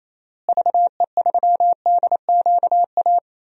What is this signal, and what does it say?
Morse code, heard as a keyed tone, about 28 wpm, 700 Hz. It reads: VE3DQA
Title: DXCC-100 Call Signs - 6 Characters - Repeated 3x - 28wpm